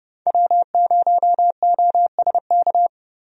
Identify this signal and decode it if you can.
Morse code, heard as a keyed tone, about 30 wpm, 700 Hz. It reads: W0OSK
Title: USA Call Signs - 5 Characters - 30wpm